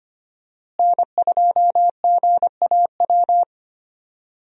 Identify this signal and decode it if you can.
Morse code, heard as a keyed tone, about 25 wpm, 700 Hz. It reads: N2GAW
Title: USA Call Signs - 5 Characters - Repeated 3x - 25wpm